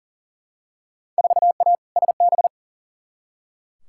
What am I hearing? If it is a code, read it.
Morse code, heard as a keyed tone, about 40 wpm, 700 Hz. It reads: 4A SB
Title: Field Day Exchanges Encore - 40wpm